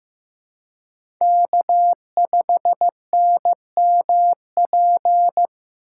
Morse code, heard as a keyed tone, about 18 wpm, 700 Hz. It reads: K5NMP
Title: Speed Racing - USA 1x3 Call Signs 15wpm